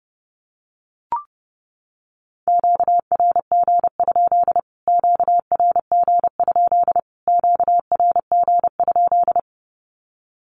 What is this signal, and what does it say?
Morse code, heard as a keyed tone, about 30 wpm, 700 Hz. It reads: QRG? QRG? QRG?
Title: Q-Codes Repeated 3x 30wpm